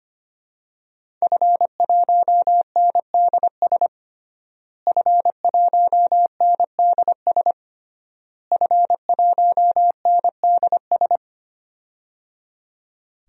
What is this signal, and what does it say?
Morse code, heard as a keyed tone, about 25 wpm, 700 Hz. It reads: F1NDS F1NDS F1NDS
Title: DXCC-100 Call Signs - 5 Characters - Repeated 3x - 25wpm